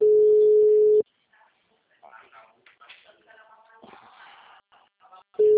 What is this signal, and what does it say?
Ringback tone (Europe, 425 Hz), with speech.
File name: out- -2034- - - .8771